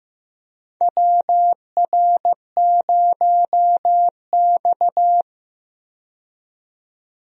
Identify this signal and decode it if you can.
Morse code, heard as a keyed tone, about 15 wpm, 700 Hz. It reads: WR0X
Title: USA Call Signs - 4 Characters - 15wpm